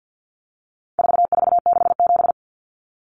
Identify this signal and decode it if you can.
Morse code, heard as a keyed tone, about 50 wpm, 700 Hz. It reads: 4467
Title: Numbers - 4-Digit 50wpm